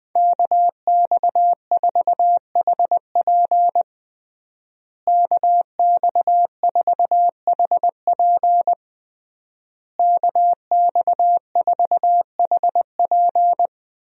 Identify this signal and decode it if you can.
Morse code, heard as a keyed tone, about 19 wpm, 700 Hz. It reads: KX4HP KX4HP KX4HP
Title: USA Call Signs - 5 Characters - Repeated 3x - 20wpm